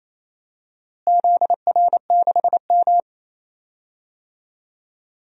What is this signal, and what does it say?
Morse code, heard as a keyed tone, about 28 wpm, 700 Hz. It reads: ZR6M
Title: DXCC-100 Call Signs - 4 Characters - Repeated 2x - 28wpm